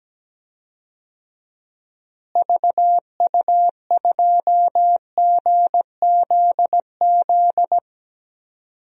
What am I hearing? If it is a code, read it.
Morse code, heard as a keyed tone, about 17 wpm, 700 Hz. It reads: VU2GZZ
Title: DXCC-100 Call Signs - 6 Characters - Repeated 3x - 17wpm